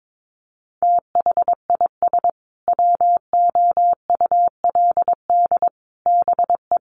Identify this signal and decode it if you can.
Morse code, heard as a keyed tone, about 22 wpm, 700 Hz. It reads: THIS WOULD BE
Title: Sentences from Top 1200 Words 22wpm